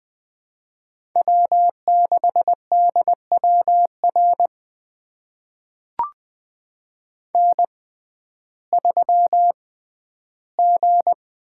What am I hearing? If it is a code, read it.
Morse code, heard as a keyed tone, about 20 wpm, spaced as for 12 wpm, 700 Hz. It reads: W6DWR N3G
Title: USA Call Signs - 5 Characters - 6x Character Spacing - 20wpm